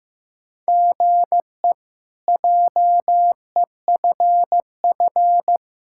Morse code, heard as a keyed tone, about 16 wpm, 700 Hz. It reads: GE JEFF
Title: K1USN SST QSO Practice - 15wpm